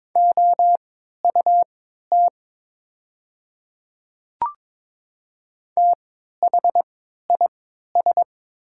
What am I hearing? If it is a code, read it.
Morse code, heard as a keyed tone, about 22 wpm, spaced as for 12 wpm, 700 Hz. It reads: OUT THIS
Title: 3x Character Spacing - Top 200 Words - 22wpm